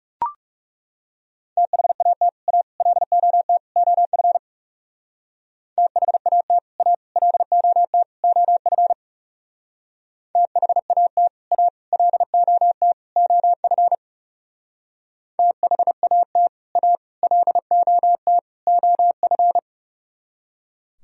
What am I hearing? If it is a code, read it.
Morse code, heard as a keyed tone, about 38 wpm, 700 Hz. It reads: THAT A LOT OF THAT A LOT OF THAT A LOT OF THAT A LOT OF
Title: Speed-Racing - Sets of 4 Words 30wpm